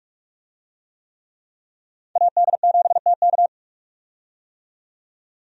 Morse code, heard as a keyed tone, about 46 wpm, 700 Hz. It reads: AD7TK
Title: USA Call Signs - 5 Characters - 45wpm